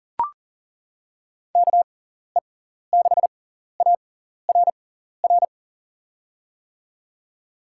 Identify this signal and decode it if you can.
Morse code, heard as a keyed tone, about 40 wpm, spaced as for 14 wpm, 700 Hz. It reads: KE6ARR
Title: USA Call Signs - 6 Characters - 6x Character Spacing - 40wpm